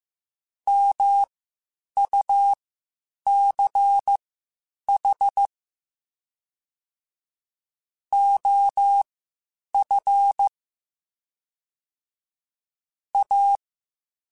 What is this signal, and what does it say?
Morse code, heard as a keyed tone, about 15 wpm, spaced as for 8 wpm, 800 Hz. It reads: MUCH OF A